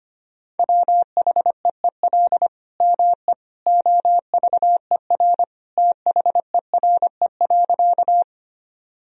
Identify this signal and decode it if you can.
Morse code, heard as a keyed tone, about 25 wpm, 700 Hz. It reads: WHEEL ME OVER THERE.
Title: Sentences from Top 1000 Words - Review Part 3 - 25wpm